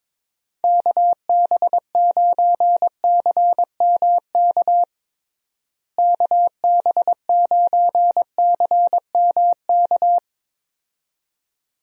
Morse code, heard as a keyed tone, about 22 wpm, 700 Hz. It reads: KB9CMK KB9CMK
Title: USA Call Signs - 6 Characters - Repeated 2x - 22wpm